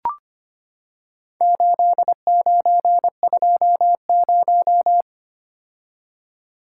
Morse code, heard as a keyed tone, about 25 wpm, 700 Hz. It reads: 8920
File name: practice / 25wpm / Four Letter-Number Combinations - 25wpm